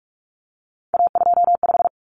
Morse code, heard as a keyed tone, about 44 wpm, 700 Hz. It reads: A25
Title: Common Cut Numbers - 3-Digit 45wpm